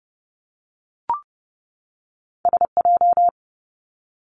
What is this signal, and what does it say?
Morse code, heard as a keyed tone, about 30 wpm, 700 Hz. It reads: S J
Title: Two Challenging Letter-Number Combinations 30wpm